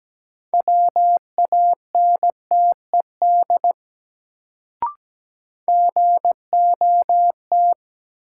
Morse code, heard as a keyed tone, about 17 wpm, 700 Hz. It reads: WANTED GOT
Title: Sets of 2 Words - New Word Drill - 17wpm